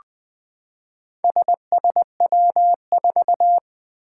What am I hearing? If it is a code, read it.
Morse code, heard as a keyed tone, about 20 wpm, 700 Hz. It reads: SSW4
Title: Four Challenging Letter-Number Combinations 20wpm